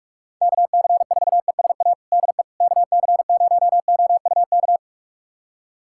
Morse code, heard as a keyed tone, about 46 wpm, 700 Hz. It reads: KC4ESA DE KC0OUK
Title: USA Callsign DE Callsign - 6 Characters - 45wpm